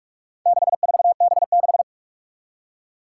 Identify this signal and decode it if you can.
Morse code, heard as a keyed tone, about 46 wpm, 700 Hz. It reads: 64B6
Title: Four Challenging Letter-Number Combinations 45wpm